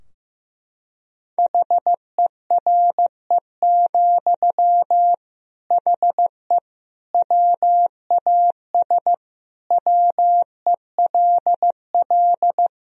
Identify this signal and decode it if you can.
Morse code, heard as a keyed tone, about 16 wpm, 700 Hz. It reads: HERE, HE WAS WELL
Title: Sentences from Top 1000 Words - Review Part 3 - 15wpm